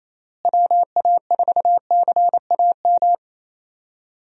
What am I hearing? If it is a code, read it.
Morse code, heard as a keyed tone, about 28 wpm, 700 Hz. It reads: WA4CAM
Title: USA Call Signs - 6 Characters - Repeated 2x - 28wpm